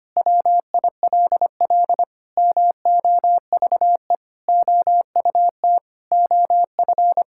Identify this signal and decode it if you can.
Morse code, heard as a keyed tone, about 25 wpm, 700 Hz. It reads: WILL MOVE OUT OF
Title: Sentences from Top 1000 Words - Review Part 6 - 25wpm